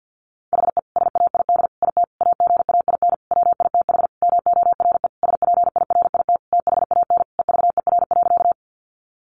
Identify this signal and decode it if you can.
Morse code, heard as a keyed tone, about 50 wpm, 700 Hz. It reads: HE SAID IT AGAIN WITH MORE SPIRIT THAN EVER.
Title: Sentences from Top 1400 Words 50wpm